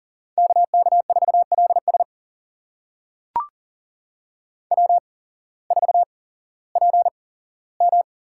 Morse code, heard as a keyed tone, about 40 wpm, spaced as for 19 wpm, 700 Hz. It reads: KK4LS W4PM
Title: USA Call Signs - 8x Character Spacing - 40wpm